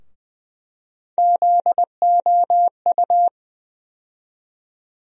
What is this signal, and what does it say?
Morse code, heard as a keyed tone, about 20 wpm, 700 Hz. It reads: ZOU
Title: USA Call Sign Postfixes - 3 Letters 20wpm